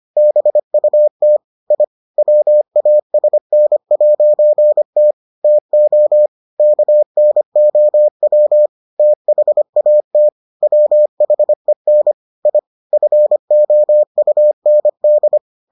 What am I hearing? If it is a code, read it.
Morse code, heard as a keyed tone, about 25 wpm, 600 Hz. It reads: BUT I WASN'T TO KNOW THAT WHEN I FOUND